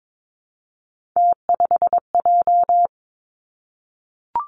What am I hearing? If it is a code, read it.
Morse code, heard as a keyed tone, about 22 wpm, 700 Hz. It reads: T5J
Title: Three Challenging Letter-Number Combinations 22wpm